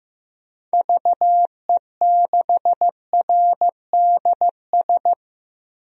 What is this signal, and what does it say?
Morse code, heard as a keyed tone, about 15 wpm, 700 Hz. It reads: VE6RDS
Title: DXCC-100 Call Signs - 6 Characters - 15wpm